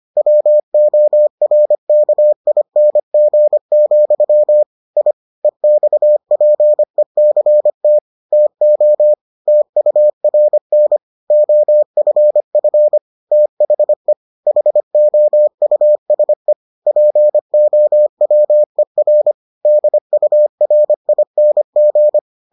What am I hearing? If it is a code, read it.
Morse code, heard as a keyed tone, about 25 wpm, 600 Hz. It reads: WORKING, I EXPECT TO TURN OFF THE HOUSE POWER DURING